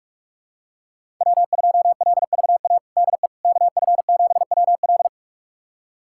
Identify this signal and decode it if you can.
Morse code, heard as a keyed tone, about 46 wpm, 700 Hz. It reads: W2LVA DE KF7WL
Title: USA Callsign DE Callsign - 5 Characters - 45wpm